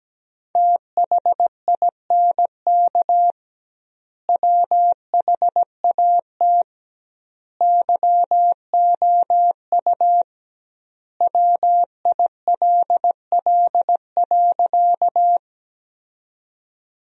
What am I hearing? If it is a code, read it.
Morse code, heard as a keyed tone, about 17 wpm, 700 Hz. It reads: THINK WHAT YOU WILL.
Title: 2x Word-Spacing - Sentences from Top 200 Words - 17wpm